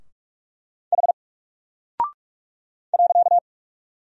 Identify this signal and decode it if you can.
Morse code, heard as a keyed tone, about 43 wpm, 700 Hz. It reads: H .
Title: Single Character (Letter:Number:Punctuation:Prosign) Alt 45wpm